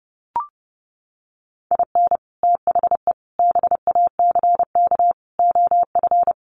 Morse code, heard as a keyed tone, about 30 wpm, 700 Hz. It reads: IN THE BACK OF
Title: Sets of 4 Words 30wpm